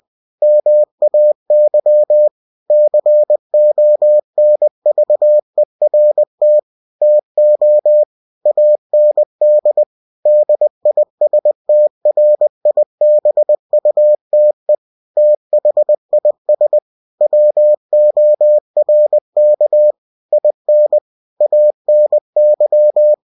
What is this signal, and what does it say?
Morse code, heard as a keyed tone, about 20 wpm, 600 Hz. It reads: MAY CONVERT TO AND DISTRIBUTE THIS WORK IN ANY
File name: cw-books / winnie-the-pooh / winnie-20wpm-page-0116